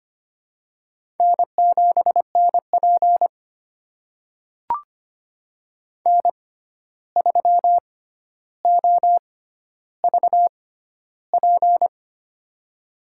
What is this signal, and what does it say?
Morse code, heard as a keyed tone, about 25 wpm, spaced as for 12 wpm, 700 Hz. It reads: N7NP N3OVP
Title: USA Call Signs - 6x Character Spacing - 25wpm